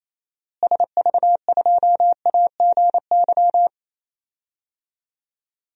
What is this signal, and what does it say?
Morse code, heard as a keyed tone, about 28 wpm, 700 Hz. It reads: SV2AGY
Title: DXCC-100 Call Signs - 6 Characters - Repeated 2x - 28wpm